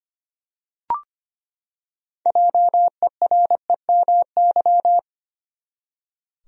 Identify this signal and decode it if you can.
Morse code, heard as a keyed tone, about 25 wpm, 700 Hz. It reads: JEREMY
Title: USA Common Names - 25wpm